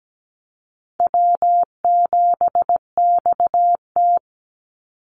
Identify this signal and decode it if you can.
Morse code, heard as a keyed tone, about 17 wpm, 700 Hz. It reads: W7XT
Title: Easy Call Signs - 17wpm